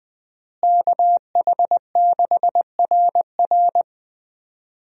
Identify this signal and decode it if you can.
Morse code, heard as a keyed tone, about 20 wpm, 700 Hz. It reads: KH6RR
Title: USA Call Signs - 5 Characters - Repeated 2x - 20wpm